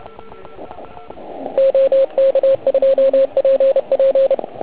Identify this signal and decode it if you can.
Morse code, heard as a keyed tone, about 30 wpm, 540 Hz. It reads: OK2PP
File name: Archiv / CUC129 / CUC129 pileup